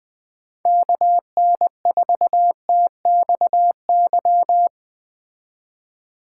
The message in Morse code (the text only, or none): KN4TXY